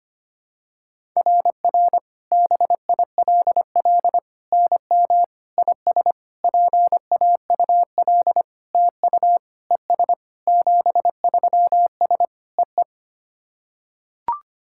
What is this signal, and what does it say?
Morse code, heard as a keyed tone, about 25 wpm, 700 Hz. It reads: RR BILL NM IS PAUL TU ES 73S EE
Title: SOTA QSO Practice 25wpm